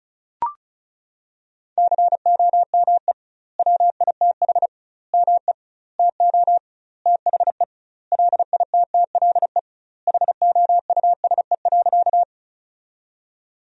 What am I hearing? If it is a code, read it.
Morse code, heard as a keyed tone, about 36 wpm, 700 Hz. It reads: COME WITH ME TO THE LITTLE HOUSE.
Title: 2x Word-Spacing - Sentences from Top 200 Words - 35wpm